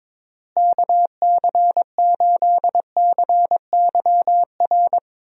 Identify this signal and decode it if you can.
Morse code, heard as a keyed tone, about 22 wpm, 700 Hz. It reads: KC8CYR
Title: USA Call Signs - 6 Characters - Repeated 3x - 22wpm